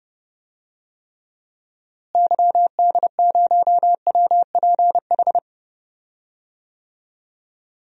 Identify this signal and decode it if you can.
Morse code, heard as a keyed tone, about 29 wpm, 700 Hz. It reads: YD0WPH
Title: DXCC-100 Call Signs - 6 Characters - Repeated 3x - 30wpm